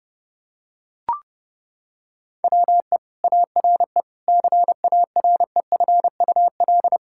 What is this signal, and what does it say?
Morse code, heard as a keyed tone, about 30 wpm, 700 Hz. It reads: WE ARE CAREFUL
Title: Sentences from Top 1600 Words 30wpm